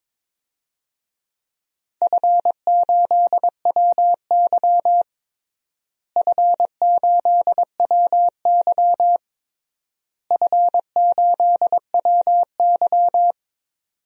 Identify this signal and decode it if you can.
Morse code, heard as a keyed tone, about 22 wpm, 700 Hz. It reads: F8WY F8WY F8WY
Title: DXCC-100 Call Signs - 4 Characters - Repeated 3x - 22wpm